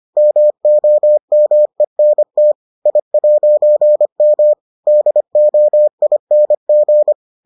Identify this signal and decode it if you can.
Morse code, heard as a keyed tone, about 25 wpm, 600 Hz. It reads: MOMENT I'M DOING